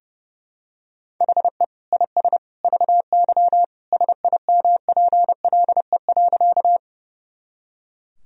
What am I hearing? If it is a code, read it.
Morse code, heard as a keyed tone, about 30 wpm, 700 Hz. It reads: HE IS VY SIMPLE.
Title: Sentences from Top 400 Words and Abbreviations 30wpm